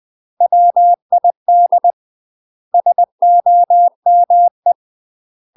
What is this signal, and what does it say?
Morse code, heard as a keyed tone, about 20 wpm, 700 Hz. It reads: WID SOME
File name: UK MASTER QSOs TRAINING / 20WPM / UK MASTER QSOs 0045